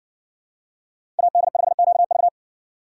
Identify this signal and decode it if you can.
Morse code, heard as a keyed tone, about 54 wpm, 700 Hz. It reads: AD5QV